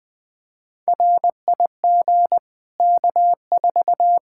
Morse code, heard as a keyed tone, about 20 wpm, 700 Hz. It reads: RIG K4
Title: Practice QSO - Exchange 2 - Info - 20wpm